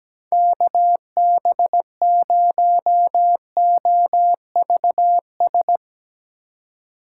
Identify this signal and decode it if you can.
Morse code, heard as a keyed tone, about 18 wpm, 700 Hz. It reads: KB0OVS
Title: Speed Racing - USA 2x3 Call Signs 17wpm